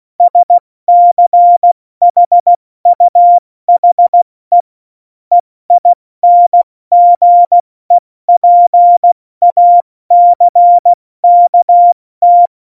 Morse code, heard as a keyed tone, about 16 wpm, 700 Hz. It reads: SCHUHE EINGEPACKT
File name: Morsekurs / Witze / Witz 24. November 2020, 70 F80